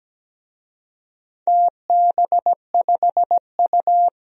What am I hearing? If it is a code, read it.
Morse code, heard as a keyed tone, about 17 wpm, 700 Hz. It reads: TB5U
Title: DXCC-100 Call Signs - 4 Characters - 17wpm